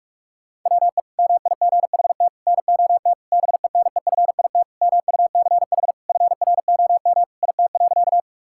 Morse code, heard as a keyed tone, about 46 wpm, 700 Hz. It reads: WE MIGHT NOT BENEFIT MUCH FROM IT.
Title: Sentences from Top 1000 Words - Review Part 6 - 45wpm